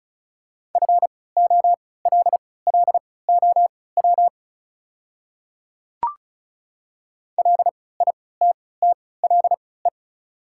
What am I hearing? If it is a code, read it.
Morse code, heard as a keyed tone, about 34 wpm, spaced as for 20 wpm, 700 Hz. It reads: FOLLOW LITTLE
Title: 3x Character Spacing - Top 200 Words - 35wpm